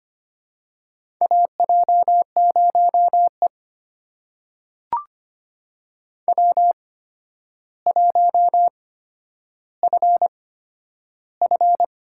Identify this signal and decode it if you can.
Morse code, heard as a keyed tone, about 25 wpm, spaced as for 11 wpm, 700 Hz. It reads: AJ0E W1FF
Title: USA Call Signs - 4 Characters - 8x Character Spacing - 25wpm